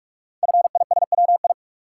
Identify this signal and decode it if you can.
Morse code, heard as a keyed tone, about 44 wpm, 700 Hz. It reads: FISWI
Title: Code Groups - 5 Letters - 45wpm